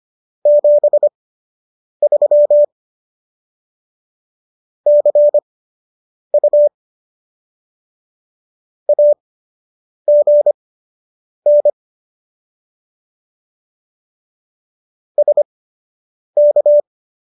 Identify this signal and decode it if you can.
Morse code, heard as a keyed tone, about 25 wpm, spaced as for 8 wpm, 590 Hz. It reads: 73 CU AGN SK